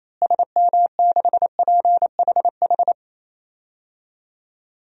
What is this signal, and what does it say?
Morse code, heard as a keyed tone, about 28 wpm, 700 Hz. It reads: SM6PHH
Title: DXCC-100 Call Signs - 6 Characters - Repeated 2x - 28wpm